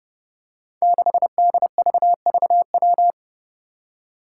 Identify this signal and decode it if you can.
Morse code, heard as a keyed tone, about 30 wpm, 700 Hz. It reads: 6DVVW
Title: Five Challenging Letter-Number Combinations 30wpm